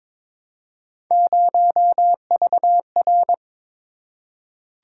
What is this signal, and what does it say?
Morse code, heard as a keyed tone, about 22 wpm, 700 Hz. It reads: 0VR
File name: practice / 22wpm / Three Letter-Number Combinations - 22wpm